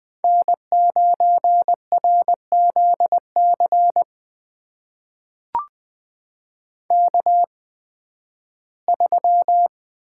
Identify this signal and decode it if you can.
Morse code, heard as a keyed tone, about 20 wpm, spaced as for 13 wpm, 700 Hz. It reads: N9RZC K3
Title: USA Call Signs - 5 Characters - 8x Character Spacing - 20wpm